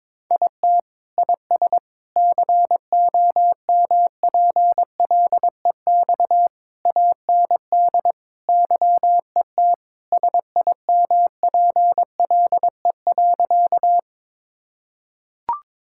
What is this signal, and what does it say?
Morse code, heard as a keyed tone, about 21 wpm, 700 Hz. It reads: IT IS COMPLEX AND YET SIMPLE.
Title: Sentences from Top 1000 Words - Review Part 6 - 22wpm